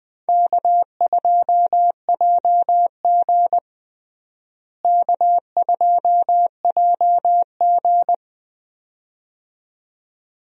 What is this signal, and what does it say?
Morse code, heard as a keyed tone, about 20 wpm, 700 Hz. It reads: K2JG K2JG
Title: USA Call Signs - 4 Characters - Repeated 2x - 20wpm